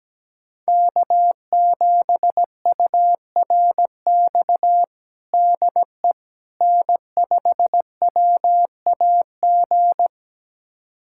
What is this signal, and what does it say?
Morse code, heard as a keyed tone, about 17 wpm, 700 Hz. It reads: K7URX DE N5WAG
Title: USA Callsign DE Callsign - 5 Characters - 17wpm